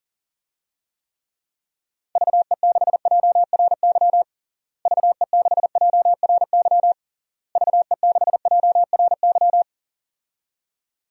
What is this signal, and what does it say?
Morse code, heard as a keyed tone, about 40 wpm, 700 Hz. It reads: VE6JRY VE6JRY VE6JRY
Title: DXCC-100 Call Signs - 6 Characters - Repeated 3x - 40wpm